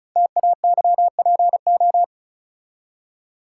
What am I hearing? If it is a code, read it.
Morse code, heard as a keyed tone, about 36 wpm, 690 Hz. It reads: TAYPO
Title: Code Groups - 5 Letters - 35wpm